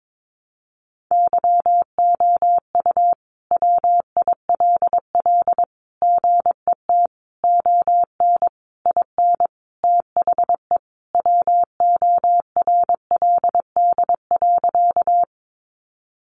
Morse code, heard as a keyed tone, about 22 wpm, 700 Hz. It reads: YOU WILL GET ON IN THE WORLD.
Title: Sentences from Top 200 Words 22wpm